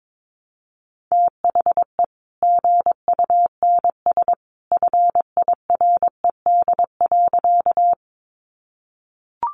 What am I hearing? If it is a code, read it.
Morse code, heard as a keyed tone, about 22 wpm, 700 Hz. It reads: THE GUNS FIRED.
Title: A Sound Of Thunder 22wpm